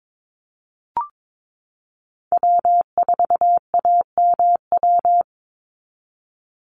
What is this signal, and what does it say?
Morse code, heard as a keyed tone, about 22 wpm, 700 Hz. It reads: W4AMW
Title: NA SOTA Chaser Callsigns 22wpm